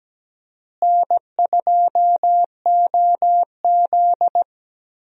Morse code, heard as a keyed tone, about 17 wpm, 700 Hz. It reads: N2OZ
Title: USA Call Signs - 4 Characters - Repeated 2x - 17wpm